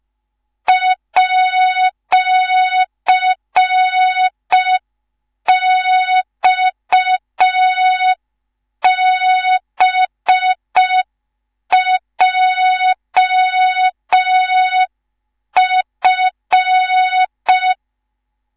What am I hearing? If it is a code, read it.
Morse code, heard as a keyed tone, about 5 wpm, 3700 Hz. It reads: @XBJF
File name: Morze-HA2MN / hang / 19 GYAKORLAT